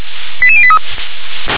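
Five-tone selective call (ZVEI), heard as quick runs of five tones, tones of about 70 ms. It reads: 88083
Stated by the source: ZVEI five-tone selective call received with an SDR.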